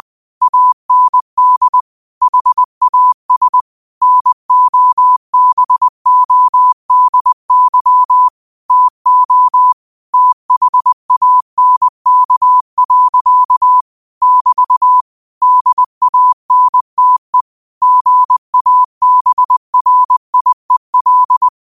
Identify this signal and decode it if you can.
Morse code, heard as a keyed tone, about 20 wpm, 1000 Hz. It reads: AND HAS NOBODY TO THANK. <BT> DANTE GABRIEL